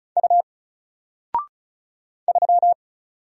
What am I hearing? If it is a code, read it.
Morse code, heard as a keyed tone, about 34 wpm, 700 Hz. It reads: U 3
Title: Single Letter-Number 35wpm